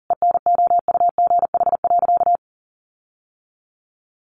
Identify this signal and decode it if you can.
Morse code, heard as a keyed tone, about 40 wpm, 700 Hz. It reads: ENOUGH.
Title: Sentences from Top 1300 Words 40wpm